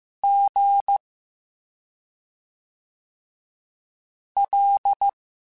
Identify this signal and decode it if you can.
Morse code, heard as a keyed tone, about 15 wpm, 800 Hz. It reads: G L